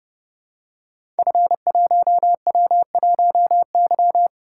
Morse code, heard as a keyed tone, about 28 wpm, 700 Hz. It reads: F1W1Y
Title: Code Groups - 5 Letter:Numbers - 30wpm